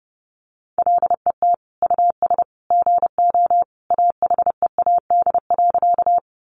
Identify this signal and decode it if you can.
Morse code, heard as a keyed tone, about 30 wpm, 700 Hz. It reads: LET US GO AHEAD.
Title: Sentences from Top 1100 Words 30wpm